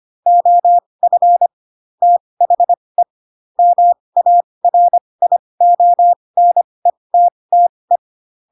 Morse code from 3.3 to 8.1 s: MARIONETTE